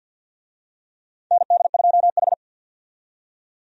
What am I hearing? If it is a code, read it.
Morse code, heard as a keyed tone, about 52 wpm, 700 Hz. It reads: ND2H
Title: USA Call Signs - 4 Characters - 50wpm